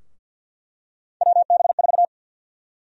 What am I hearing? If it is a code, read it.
Morse code, heard as a keyed tone, about 46 wpm, 700 Hz. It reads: WB4
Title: USA Call Sign Prefixes - Repeated 2x - 50wpm